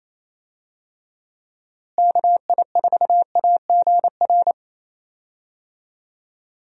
Morse code, heard as a keyed tone, about 28 wpm, 700 Hz. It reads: KI4AGR
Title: USA Call Signs - 6 Characters - Repeated 2x - 28wpm